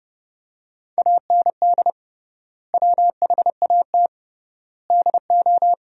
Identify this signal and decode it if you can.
Morse code, heard as a keyed tone, about 30 wpm, 700 Hz. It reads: AND WHAT DO
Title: 3x Word-Spacing - Sentences from Top 100 Words - 30wpm